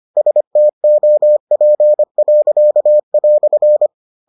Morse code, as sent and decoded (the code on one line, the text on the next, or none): ... - --- .--. .-.-.- .-..-.
STOP."